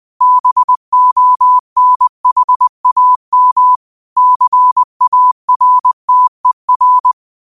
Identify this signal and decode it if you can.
Morse code, heard as a keyed tone, about 20 wpm, 1000 Hz. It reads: BONHAM CARTER
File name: qotdfiles / 20wpm